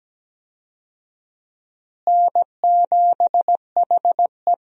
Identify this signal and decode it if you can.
Morse code, heard as a keyed tone, about 17 wpm, 700 Hz. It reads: N7HE
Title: USA Call Signs - 4 Characters - Repeated 3x - 17wpm